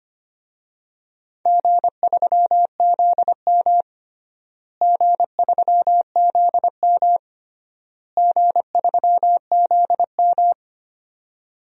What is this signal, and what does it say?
Morse code, heard as a keyed tone, about 25 wpm, 700 Hz. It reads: G3ZM G3ZM G3ZM
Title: DXCC-100 Call Signs - 4 Characters - Repeated 3x - 25wpm